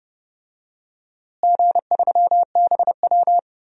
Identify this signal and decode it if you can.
Morse code, heard as a keyed tone, about 30 wpm, 700 Hz. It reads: G3BW
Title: DXCC-100 Call Signs - 4 Characters - Repeated 3x - 30wpm